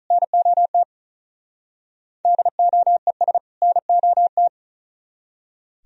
Morse code, heard as a keyed tone, about 46 wpm, 700 Hz. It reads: NOT DOES NOT